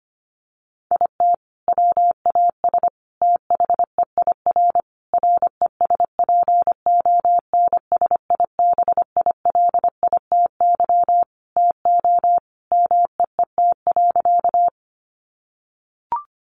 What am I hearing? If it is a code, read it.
Morse code, heard as a keyed tone, about 25 wpm, 700 Hz. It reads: IT WAS THEIR RESPONSIBILITY TO MEET.
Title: Sentences from Top 1200 Words 25wpm